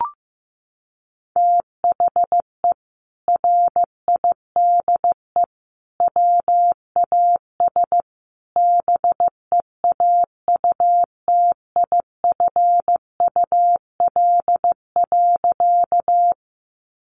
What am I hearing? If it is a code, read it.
Morse code, heard as a keyed tone, about 15 wpm, 700 Hz. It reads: THE RIDE WAS BEAUTIFUL.
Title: Sentences from Top 1200 Words 15wpm